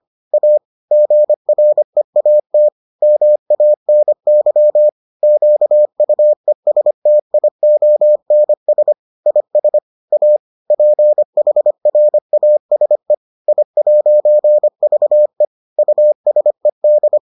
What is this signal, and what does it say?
Morse code, heard as a keyed tone, about 25 wpm, 600 Hz. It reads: A GREAT MANY QUESTIONS IS A PHRASE I'VE USED